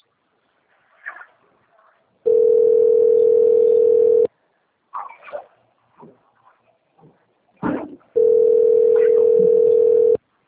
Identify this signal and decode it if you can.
Ringback tone (North America), 440 + 480 Hz.